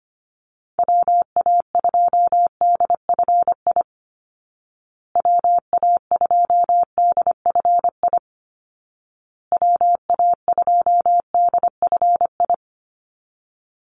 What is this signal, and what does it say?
Morse code, heard as a keyed tone, about 25 wpm, 700 Hz. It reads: WA2DFI WA2DFI WA2DFI
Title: Call Signs Repeated 2x - 25wpm